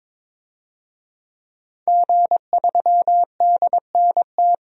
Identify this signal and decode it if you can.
Morse code, heard as a keyed tone, about 22 wpm, 700 Hz. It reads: G3DNT
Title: DXCC-100 Call Signs - 5 Characters - Repeated 3x - 22wpm